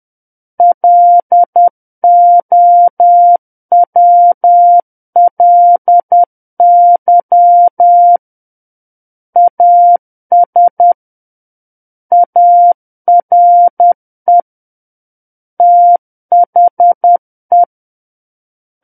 Morse code, heard as a keyed tone, about 10 wpm, 700 Hz. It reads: LOWLY AS ARE THE
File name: Book One / 01-Chapter 1 - 10 WPM